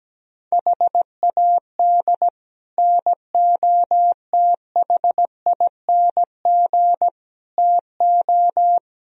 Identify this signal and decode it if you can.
Morse code, heard as a keyed tone, about 17 wpm, 700 Hz. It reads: HAD NOTHING TO
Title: Speed-Racing - Sets of 4 Words 17wpm